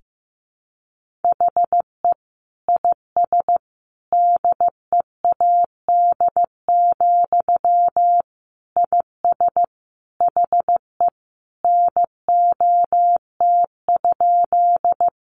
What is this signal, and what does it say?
Morse code, heard as a keyed tone, about 15 wpm, 700 Hz. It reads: HE IS DEAD, IS HE NOT?
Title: Sentences from Top 1500 Words 15wpm